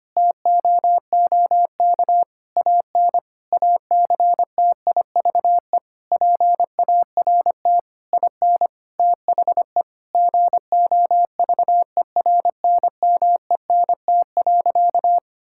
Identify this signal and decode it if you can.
Morse code, heard as a keyed tone, about 25 wpm, 700 Hz. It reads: TOOK AN ACTIVE PART IN THE GOVERNMENT.
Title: Sentences from Top 1000 Words - Review Part 6 - 25wpm